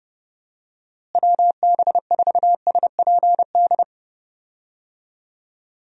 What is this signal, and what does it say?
Morse code, heard as a keyed tone, about 30 wpm, 700 Hz. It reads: WB4SPD
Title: USA Call Signs - 6 Characters - Repeated 2x - 30wpm